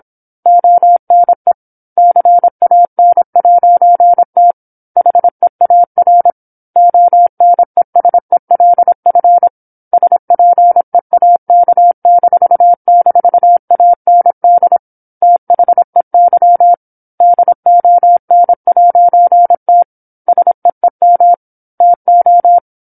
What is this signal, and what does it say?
Morse code, heard as a keyed tone, about 27 wpm, 700 Hz. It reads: ONE CAN'T HEAR ONESELF SPEAK--AND THEY DON'T SEEM TO
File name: 08-Chapter-08 26 WPM